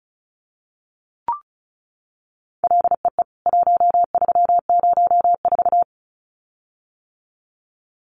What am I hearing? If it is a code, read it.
Morse code, heard as a keyed tone, about 34 wpm, 700 Hz. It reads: LEE 1304
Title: CWT Exchange 35wpm